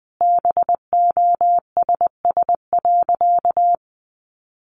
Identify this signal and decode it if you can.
Morse code, heard as a keyed tone, about 20 wpm, 700 Hz. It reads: BOSS.
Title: Sentences from Top 700 Words 20wpm